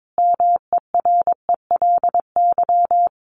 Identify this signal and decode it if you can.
Morse code, heard as a keyed tone, about 22 wpm, 700 Hz. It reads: MERELY
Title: Sentences from Top 1400 Words 22wpm